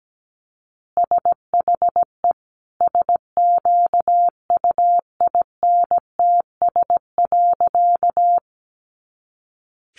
Morse code, heard as a keyed tone, about 17 wpm, 700 Hz. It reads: SHE SQUINTS.
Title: Additional Easy Phrases 17wpm